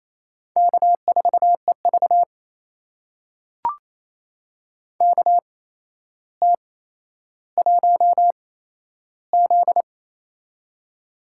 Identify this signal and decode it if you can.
Morse code, heard as a keyed tone, about 28 wpm, spaced as for 12 wpm, 700 Hz. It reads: K4EV KT1Z
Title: USA Call Signs - 4 Characters - 8x Character Spacing - 28wpm